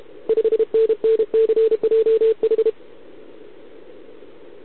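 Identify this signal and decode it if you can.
Morse code, heard as a keyed tone, about 31 wpm, 440 Hz. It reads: 5NNCJH
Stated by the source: Contest CW from OK1RI on 80 m during the OKOM DX contest, recorded by a QRP station using a KX3 and a wire antenna.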